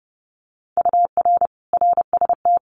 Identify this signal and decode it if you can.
Morse code, heard as a keyed tone, about 30 wpm, 700 Hz. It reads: UR RST
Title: Common QSO Phrases Encore 30wpm